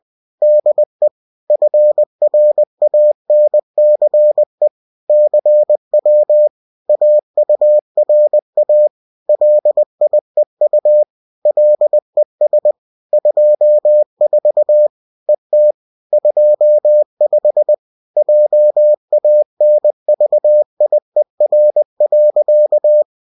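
Morse code, heard as a keyed tone, about 20 wpm, 600 Hz. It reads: DE FRANCE CW AURA LIEU LES 24 ET 25 JANVIER.
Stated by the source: UFT bulletin sent in CW.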